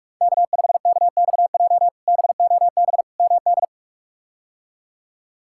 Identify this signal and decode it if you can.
Morse code, heard as a keyed tone, about 42 wpm, 700 Hz. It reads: K5KXJ BOB MD
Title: K1USN SST QSO Practice - 45wpm